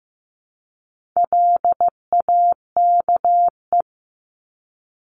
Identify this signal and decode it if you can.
Morse code, heard as a keyed tone, about 15 wpm, 700 Hz. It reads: LAKE
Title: Top 1001-1250 Words 15wpm